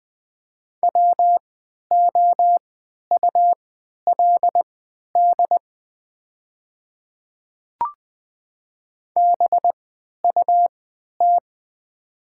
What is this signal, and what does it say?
Morse code, heard as a keyed tone, about 20 wpm, spaced as for 11 wpm, 700 Hz. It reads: WOULD BUT
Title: 3x Character Spacing - Top 100 Words - 20wpm